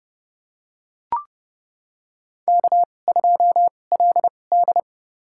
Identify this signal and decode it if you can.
Morse code, heard as a keyed tone, about 30 wpm, spaced as for 22 wpm, 700 Hz. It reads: K2LD
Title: USA Call Signs - 4 Characters - 2x Character Spacing - 30wpm